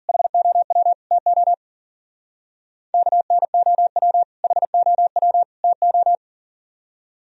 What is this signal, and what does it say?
Morse code, heard as a keyed tone, about 47 wpm, 700 Hz. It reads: HOW TO KNOW HOW TO
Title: Speed-Racing - Sets of 3 Words 40wpm